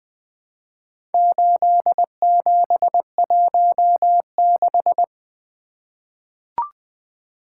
Morse code, heard as a keyed tone, about 20 wpm, 700 Hz. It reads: 8716
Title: Four Letter-Number Combinations - 20wpm